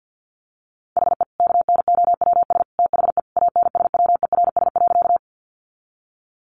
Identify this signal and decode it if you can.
Morse code, heard as a keyed tone, about 50 wpm, 700 Hz. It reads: HE KNOWS THE ANSWERS.
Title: Additional Easy Phrases 50wpm